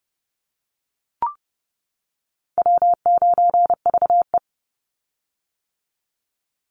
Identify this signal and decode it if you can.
Morse code, heard as a keyed tone, about 30 wpm, 700 Hz. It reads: W9VE
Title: Easy Call Signs - 30wpm